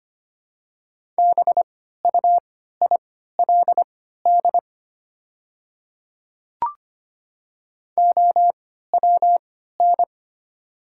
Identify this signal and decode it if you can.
Morse code, heard as a keyed tone, about 25 wpm, spaced as for 14 wpm, 690 Hz. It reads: BUILD OWN
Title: 3x Character Spacing - Top 300 Words - 25wpm